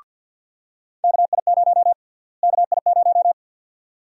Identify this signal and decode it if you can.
Morse code, heard as a keyed tone, about 51 wpm, 700 Hz. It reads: KI0 KI0
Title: USA Call Sign Prefixes - Repeated 2x - 50wpm